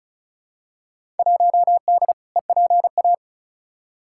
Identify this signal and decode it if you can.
Morse code, heard as a keyed tone, about 36 wpm, 700 Hz. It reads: 1D EPA
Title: Field Day Exchanges - 35wpm